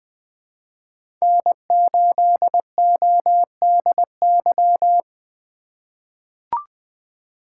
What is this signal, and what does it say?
Morse code, heard as a keyed tone, about 20 wpm, 700 Hz. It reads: N8ODY
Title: USA Call Signs - 5 Characters - 3x Character Spacing - 20wpm